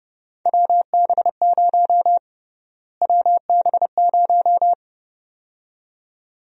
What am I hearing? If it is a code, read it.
Morse code, heard as a keyed tone, about 30 wpm, 700 Hz. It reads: WB0 WB0
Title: USA Call Sign Prefixes - Repeated 2x - 30wpm